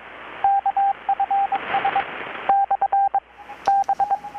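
Morse code, heard as a keyed tone, about 22 wpm, 790 Hz, with radio noise. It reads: KFS / B
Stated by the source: Morse from KPH's Night of Nights XX maritime radiotelegraphy event, received in Chiapas, Mexico about 3,800 km away on a Kenwood R-600 with a grounded 10 m long-wire antenna.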